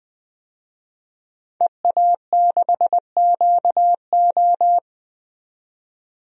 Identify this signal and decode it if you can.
Morse code, heard as a keyed tone, about 20 wpm, 700 Hz. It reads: EA6QO
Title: DXCC-100 Call Signs - 5 Characters - 20wpm